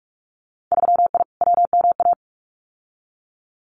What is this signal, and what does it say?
Morse code, heard as a keyed tone, about 46 wpm, 690 Hz. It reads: 3I WMA
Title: Winter Field Day Exchanges Encore 45wpm